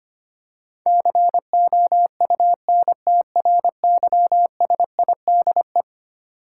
Morse code, heard as a keyed tone, about 25 wpm, 700 Hz. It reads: COUNTRYSIDE
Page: morse-code-ninja-practice-25wpm